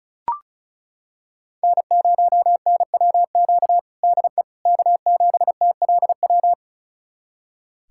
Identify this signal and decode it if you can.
Morse code, heard as a keyed tone, about 34 wpm, 700 Hz. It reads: N0NWQ DE K7TLW
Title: USA Callsign DE Callsign - 5 Characters - 35wpm